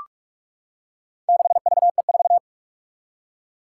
Morse code, heard as a keyed tone, about 44 wpm, 700 Hz. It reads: 6VE4
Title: Four Challenging Letter-Number Combinations 45wpm